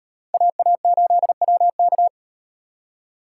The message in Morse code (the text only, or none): AA8WK